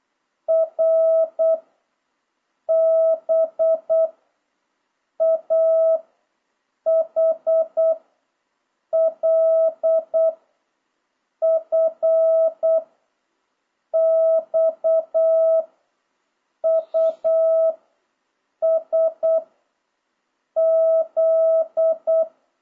Morse code, heard as a keyed tone, about 8 wpm, 650 Hz. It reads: R B A H L F X U S Z
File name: jh8cbh / 2026cw